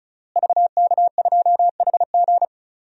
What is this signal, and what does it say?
Morse code, heard as a keyed tone, about 34 wpm, 700 Hz. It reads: VK2HG
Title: DXCC-100 Call Signs - 5 Characters - Repeated 3x - 35wpm